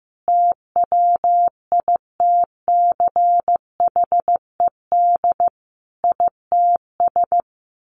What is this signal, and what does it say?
Morse code, heard as a keyed tone, about 14 wpm, 700 Hz. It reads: TWITCHED ITS
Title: A Sound Of Thunder 15wpm